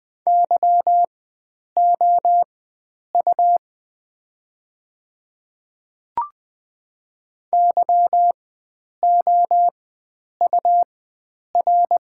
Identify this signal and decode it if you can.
Morse code, heard as a keyed tone, about 20 wpm, spaced as for 9 wpm, 700 Hz. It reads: YOU YOUR
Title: 4x Character Spacing - Top 500 Words - 20wpm